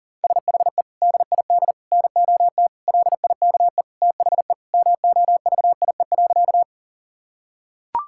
Morse code, heard as a keyed tone, about 40 wpm, 700 Hz. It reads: SHE DID NOT LIKE THE MOVIE.
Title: Sentences from Top 1000 Words - Review Part 5 - 40wpm